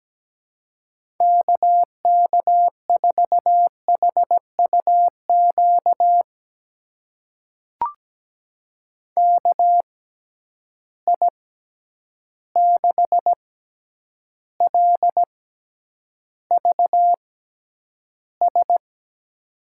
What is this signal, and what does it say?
Morse code, heard as a keyed tone, about 17 wpm, spaced as for 9 wpm, 700 Hz. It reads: KK4HUQ KI6LVS
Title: USA Call Signs - 6 Characters - 6x Character Spacing - 17wpm